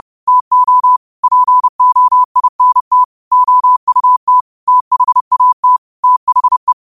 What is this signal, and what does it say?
Morse code, heard as a keyed tone, about 30 wpm, 1000 Hz. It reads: TO POINT OUT THAT THE
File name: qotdfiles / 30wpm